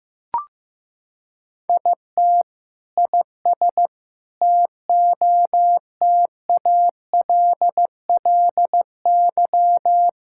Morse code, heard as a keyed tone, about 15 wpm, 700 Hz. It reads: IT IS TOTALLY
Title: Sentences from Top 1600 Words 15wpm